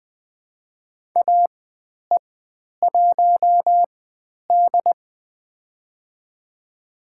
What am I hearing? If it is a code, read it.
Morse code, heard as a keyed tone, about 20 wpm, spaced as for 10 wpm, 700 Hz. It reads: AE1D
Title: 102 Lesson 17g - Letter u - Call Signs 20wpm